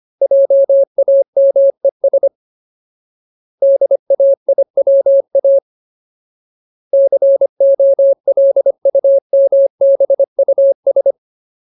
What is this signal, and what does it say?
Morse code, heard as a keyed tone, about 25 wpm, 550 Hz. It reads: JAMES DAIWA COLUMBUS
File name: Ultimate Word List / 25WPM Ultimate